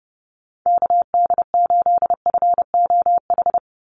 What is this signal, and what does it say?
Morse code, heard as a keyed tone, about 30 wpm, 700 Hz. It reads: KD8FOH
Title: USA Call Signs 30wpm